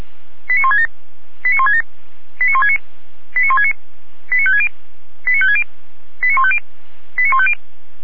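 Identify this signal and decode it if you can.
Five-tone selective call (ZVEI), heard as quick runs of five tones, tones of about 80 ms. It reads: 89167 89167 89169 89169 89560 89560 89150 89150
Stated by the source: ZVEI five-tone selective call to address 89220, received on an SDR.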